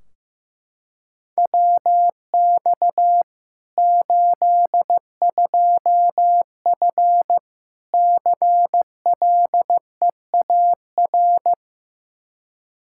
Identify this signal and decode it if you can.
Morse code, heard as a keyed tone, about 16 wpm, 700 Hz. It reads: WX 82F CLEAR
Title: Practice QSO - Exchange 2 - Info - 15wpm